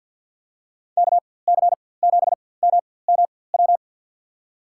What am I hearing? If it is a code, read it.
Morse code, heard as a keyed tone, about 52 wpm, spaced as for 23 wpm, 700 Hz. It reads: KC7MMW
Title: USA Call Signs - 4x Character Spacing - 50wpm